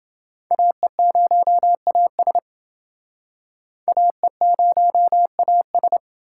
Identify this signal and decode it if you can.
Morse code, heard as a keyed tone, about 29 wpm, 700 Hz. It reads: AE0AS AE0AS